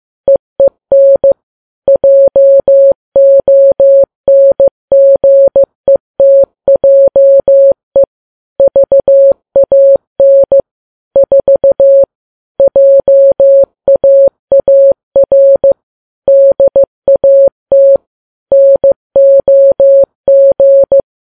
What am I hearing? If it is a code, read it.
Morse code, heard as a keyed tone, about 15 wpm, 550 Hz. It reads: EEN JONGETJE VAN 4 JAAR DAT NOG
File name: Files Web tls / Morse on9fv / morse15kps